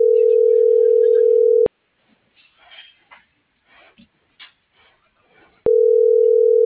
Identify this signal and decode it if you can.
Ringback tone (North America), 440 + 480 Hz.